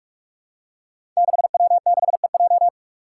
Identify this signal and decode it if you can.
Morse code, heard as a keyed tone, about 44 wpm, 700 Hz. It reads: 6W6EJ
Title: Five Challenging Letter-Number Combinations 45wpm